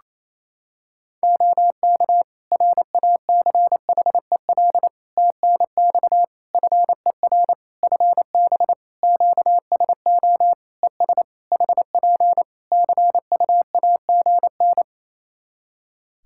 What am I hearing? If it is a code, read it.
Morse code, heard as a keyed tone, about 28 wpm, 700 Hz. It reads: OK RACHEL TNX FER FB QSO ES HP CUAGN
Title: K1USN SST QSO Practice 28wpm